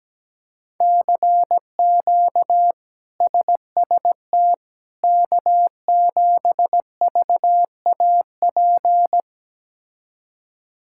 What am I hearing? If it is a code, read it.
Morse code, heard as a keyed tone, about 17 wpm, 700 Hz. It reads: CQ SST K7VAP
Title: K1USN SST QSO Practice - 17wpm